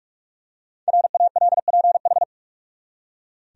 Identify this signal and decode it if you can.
Morse code, heard as a keyed tone, about 44 wpm, 700 Hz. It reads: RALPH
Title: USA Common Names - 45wpm